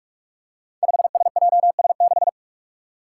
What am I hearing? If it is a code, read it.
Morse code, heard as a keyed tone, about 44 wpm, 700 Hz. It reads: 5SJS6
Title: Five Challenging Letter-Number Combinations 45wpm